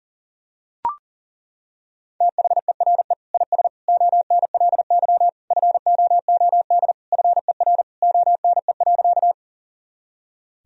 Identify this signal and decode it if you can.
Morse code, heard as a keyed tone, about 40 wpm, 700 Hz. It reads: THERE IS ONLY FOOD FER ONE.
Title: Sentences from Top 300 Words and Abbreviations 40wpm